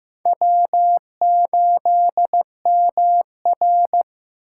Morse code, heard as a keyed tone, about 15 wpm, 700 Hz. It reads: W8MR
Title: USA Call Signs - 4 Characters - 8x Character Spacing - 15wpm